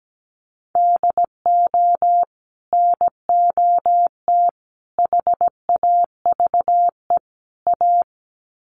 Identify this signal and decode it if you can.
Morse code, heard as a keyed tone, about 17 wpm, 700 Hz. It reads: DO NOT HAVE A
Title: Sets of 4 Words 17wpm